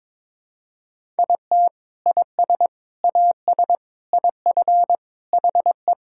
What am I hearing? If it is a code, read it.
Morse code, heard as a keyed tone, about 22 wpm, 700 Hz. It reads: IT IS AS IF HE
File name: practice / 22wpm / Sentences with 3-Word Phrases - 22wpm